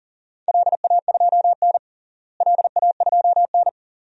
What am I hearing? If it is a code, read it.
Morse code, heard as a keyed tone, about 40 wpm, 700 Hz. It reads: LA2N LA2N
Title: DXCC-100 Call Signs - 4 Characters - Repeated 2x - 40wpm